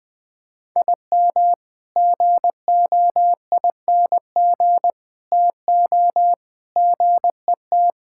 Morse code, heard as a keyed tone, about 21 wpm, 700 Hz. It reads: IM GOING TO GET
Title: Sentences with 3-Word Phrases - 20wpm